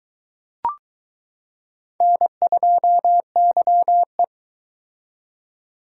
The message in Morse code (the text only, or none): N2YE